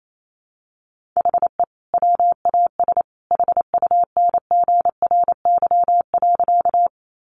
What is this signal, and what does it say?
Morse code, heard as a keyed tone, about 28 wpm, 700 Hz. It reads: HE WAS HUNGRY.
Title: Sentences from Top 1400 Words 28wpm